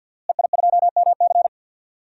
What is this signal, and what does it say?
Morse code, heard as a keyed tone, about 50 wpm, 700 Hz. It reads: EI2MC
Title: DXCC-100 Call Signs - 5 Characters - Repeated 2x - 50wpm